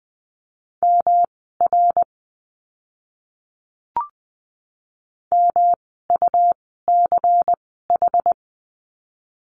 Morse code, heard as a keyed tone, about 20 wpm, spaced as for 14 wpm, 700 Hz. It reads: MR MUCH
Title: 108 Lesson 18f - Letter m - Words 20wpm